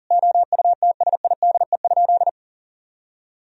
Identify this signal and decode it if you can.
Morse code, heard as a keyed tone, about 40 wpm, 700 Hz. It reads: OUTSIDE?
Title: Sentences from Top 1000 Words - Review Part 4 - 40wpm